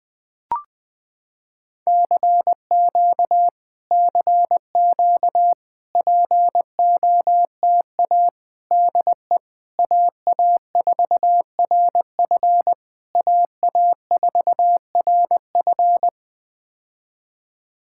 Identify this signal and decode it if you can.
Morse code, heard as a keyed tone, about 20 wpm, 700 Hz. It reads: CQ CQ POTA DE AA4RF AA4RF
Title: POTA QSO Practice - 20wpm